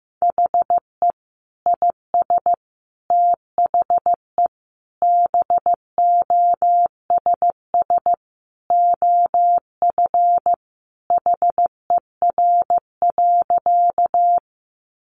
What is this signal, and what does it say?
Morse code, heard as a keyed tone, about 15 wpm, 700 Hz. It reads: HE IS THE BOSS OF HER.
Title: Sentences from Top 700 Words 15wpm